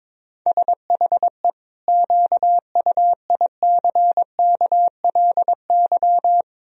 Morse code, heard as a keyed tone, about 22 wpm, 700 Hz. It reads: SHE QUICKLY
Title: Sentences from Top 1000 Words - Review Part 4 - 22wpm